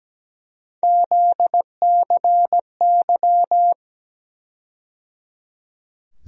Morse code, heard as a keyed tone, about 17 wpm, 700 Hz. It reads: ZCY
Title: USA Call Sign Postfixes - 3 Letters 17wpm